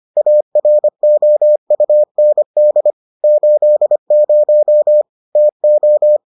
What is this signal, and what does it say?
Morse code, heard as a keyed tone, about 25 wpm, 600 Hz. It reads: AROUND 80 TO